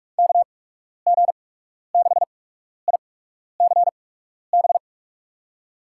Morse code, heard as a keyed tone, about 46 wpm, spaced as for 12 wpm, 700 Hz. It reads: KG6ICB
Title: USA Call Signs - 6 Characters - 8x Character Spacing - 45wpm